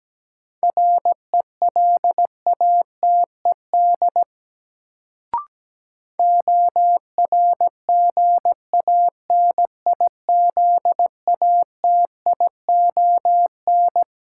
Morse code, heard as a keyed tone, about 17 wpm, 700 Hz. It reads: RELATED ORGANIZATION
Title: Sentences from Top 1000 Words - Review Part 5 - New Word Drill - 17wpm